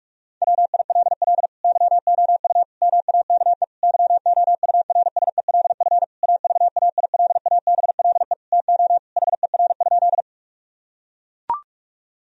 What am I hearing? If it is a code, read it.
Morse code, heard as a keyed tone, about 41 wpm, 700 Hz. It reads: WILL YOU MAKE YOURSELF AVAILABLE TO HER?
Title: Sentences from Top 1000 Words - Review Part 3 - 45wpm